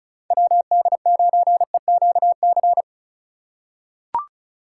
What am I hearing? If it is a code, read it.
Morse code, heard as a keyed tone, about 34 wpm, 700 Hz. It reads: WD9EQC
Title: USA Call Signs - 6 Characters - 2x Character Spacing - 35wpm